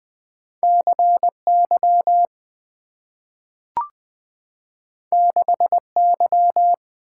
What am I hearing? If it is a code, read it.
Morse code, heard as a keyed tone, about 20 wpm, 700 Hz. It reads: C Y 6 Y
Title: Two Letter-Number Combinations - 20wpm